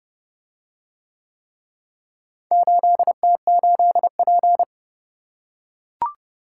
Morse code, heard as a keyed tone, about 28 wpm, 700 Hz. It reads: 8T8P